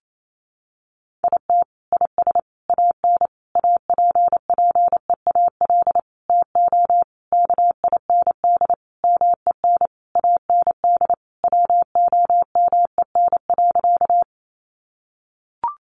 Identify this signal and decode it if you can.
Morse code, heard as a keyed tone, about 28 wpm, 700 Hz. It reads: IT IS AN APPEAL TO KIND MEN AND WOMEN.
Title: Sentences from Top 1300 Words 28wpm